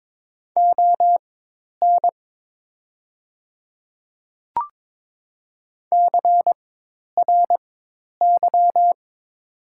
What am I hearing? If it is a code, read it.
Morse code, heard as a keyed tone, about 22 wpm, spaced as for 10 wpm, 700 Hz. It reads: ON CRY
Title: 4x Character Spacing - Top 500 Words - 22wpm